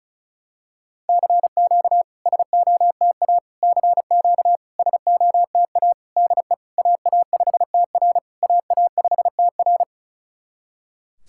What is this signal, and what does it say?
Morse code, heard as a keyed tone, about 36 wpm, 700 Hz. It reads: CQ SOTA CQ SOTA DE AA5TR AA5TR
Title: SOTA QSO Practice 35wpm